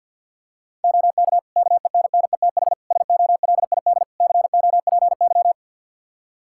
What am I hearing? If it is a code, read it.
Morse code, heard as a keyed tone, about 51 wpm, 700 Hz. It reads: OK KENNETH SOLID COPY
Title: Practice QSO - Exchange 3 - More Info - 50wpm